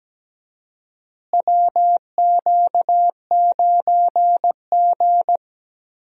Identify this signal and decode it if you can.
Morse code, heard as a keyed tone, about 17 wpm, 700 Hz. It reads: WQ9G
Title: USA Call Signs - 4 Characters - Repeated 3x - 17wpm